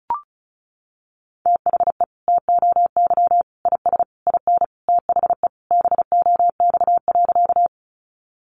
Morse code, heard as a keyed tone, about 34 wpm, 700 Hz. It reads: THE TOY IS IN THE BOX.
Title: Easy Phrases 35wpm